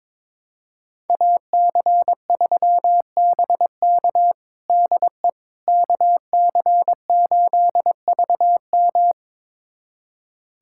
Morse code, heard as a keyed tone, about 22 wpm, 700 Hz. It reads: AC3BK DE KC8VM
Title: USA Callsign DE Callsign - 5 Characters - 22wpm